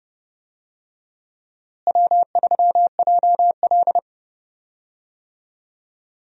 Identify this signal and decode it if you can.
Morse code, heard as a keyed tone, about 31 wpm, 700 Hz. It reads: W3JL
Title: USA Call Signs - 4 Characters - Repeated 3x - 30wpm